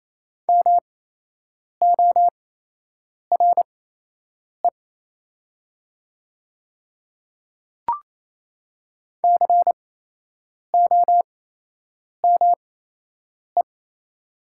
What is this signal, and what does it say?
Morse code, heard as a keyed tone, about 28 wpm, spaced as for 8 wpm, 700 Hz. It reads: MORE COME
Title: 8x Character Spacing - Top 100 Words - 28wpm